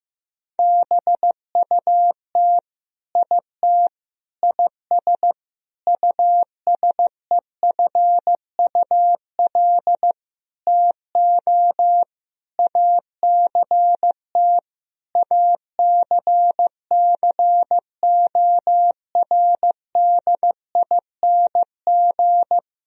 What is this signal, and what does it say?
Morse code, heard as a keyed tone, about 15 wpm, 700 Hz. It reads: BUT IT IS USEFUL TO ACT ACCORDING
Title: Sentences from Top 1000 Words - Review Part 4 - 15wpm